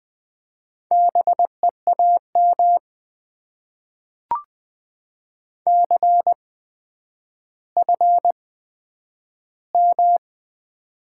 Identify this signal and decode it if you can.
Morse code, heard as a keyed tone, about 20 wpm, spaced as for 10 wpm, 700 Hz. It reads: BEAM CFM
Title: Top 90 QSO Elements - 8x Character Spacing - 20wpm